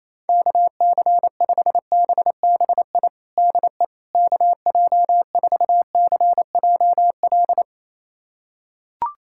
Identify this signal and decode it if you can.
Morse code, heard as a keyed tone, about 28 wpm, 700 Hz. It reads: KC5BBI DE KJ4CJL
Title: USA Callsign DE Callsign - 6 Characters - 28wpm